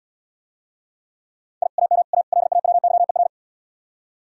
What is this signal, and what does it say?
Morse code, heard as a keyed tone, about 76 wpm, 700 Hz. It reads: I AM A LAWYER